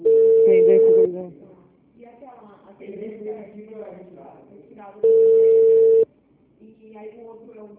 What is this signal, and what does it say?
Ringback tone (China, 450 Hz), with speech.